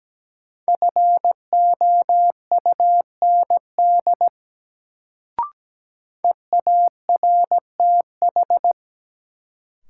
Morse code, heard as a keyed tone, about 17 wpm, 700 Hz. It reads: FOUND EARTH
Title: Sets of 5 Words Encore - New Word Drill - 17wpm